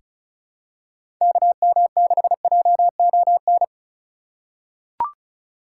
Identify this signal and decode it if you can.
Morse code, heard as a keyed tone, about 36 wpm, 700 Hz. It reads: KM6JON
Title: USA Call Signs - 6 Characters - 3x Character Spacing - 35wpm